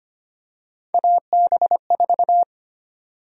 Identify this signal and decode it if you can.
Morse code, heard as a keyed tone, about 25 wpm, 700 Hz. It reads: AB4
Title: USA Call Sign Prefixes - 25wpm